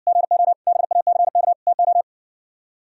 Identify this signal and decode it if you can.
Morse code, heard as a keyed tone, about 59 wpm, 700 Hz. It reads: GO BACK TO